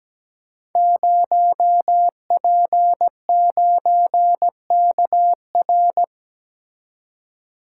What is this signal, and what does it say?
Morse code, heard as a keyed tone, about 17 wpm, 700 Hz. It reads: 0P9KR
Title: Code Groups - 5 Letter:Numbers - 17wpm